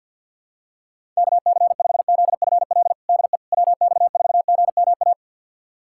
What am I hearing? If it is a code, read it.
Morse code, heard as a keyed tone, about 50 wpm, 700 Hz. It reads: KC5ZFL DE WX4GMA
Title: USA Callsign DE Callsign - 6 Characters - 50wpm